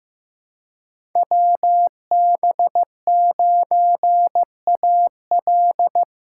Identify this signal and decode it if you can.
Morse code, heard as a keyed tone, about 15 wpm, 700 Hz. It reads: WB9AL
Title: USA Call Signs - 5 Characters - 4x Character Spacing - 15wpm